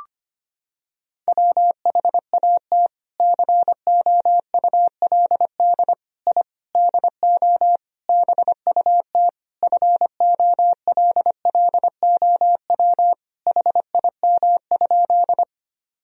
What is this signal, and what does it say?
Morse code, heard as a keyed tone, about 25 wpm, 700 Hz. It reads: WHAT COULD I DO BUT FOLLOW HIM?
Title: Sentences from Top 1000 Words - Review Part 1 - 25wpm